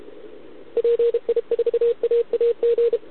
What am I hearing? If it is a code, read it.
Morse code, heard as a keyed tone, about 33 wpm, 470 Hz. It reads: PI4AAG